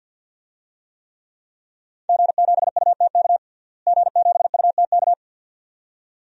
Morse code, heard as a keyed tone, about 51 wpm, 700 Hz. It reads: G7UTK G7UTK
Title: DXCC-100 Call Signs - 5 Characters - Repeated 2x - 50wpm